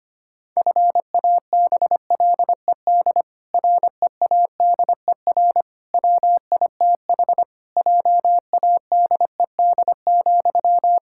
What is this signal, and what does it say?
Morse code, heard as a keyed tone, about 25 wpm, 700 Hz. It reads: FABLED READER WITH JADED,
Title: Pangram Challenge v2 - 25wpm